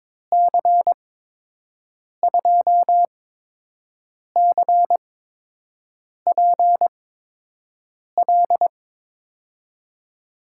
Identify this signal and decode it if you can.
Morse code, heard as a keyed tone, about 21 wpm, spaced as for 6 wpm, 700 Hz. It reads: C2CPL
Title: USA Call Signs - 6 Characters - 8x Character Spacing - 22wpm